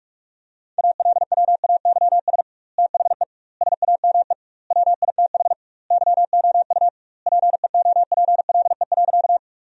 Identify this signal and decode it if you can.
Morse code, heard as a keyed tone, about 45 wpm, 700 Hz. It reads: ALWAYS THE SAME WITH YOU PEOPLE.
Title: 2x Word-Spacing - Sentences from Top 300 Words - 45wpm